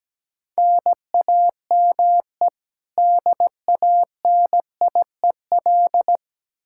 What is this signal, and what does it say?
Morse code, heard as a keyed tone, about 17 wpm, 700 Hz. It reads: NAME DANIEL
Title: Practice QSO - Exchange 1 - Beginning - 17wpm